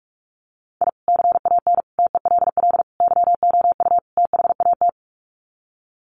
Morse code, heard as a keyed tone, about 44 wpm, 700 Hz. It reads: I CAN TELL YOU THAT
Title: Sets of 5 Words 45wpm